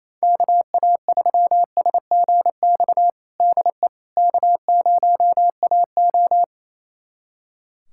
Morse code, heard as a keyed tone, about 27 wpm, 700 Hz. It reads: KA3SGX DE K0AO
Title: USA Callsign DE Callsign 28wpm